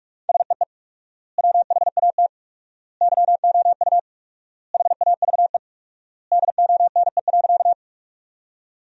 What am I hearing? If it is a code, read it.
Morse code, heard as a keyed tone, about 43 wpm, 700 Hz. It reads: SEE WHAT YOU HAVE DONE.
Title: 4x Word-Spacing - Sentences from Top 500 Words - 45wpm